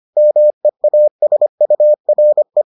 Morse code, heard as a keyed tone, about 25 wpm, 600 Hz. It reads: MEASURE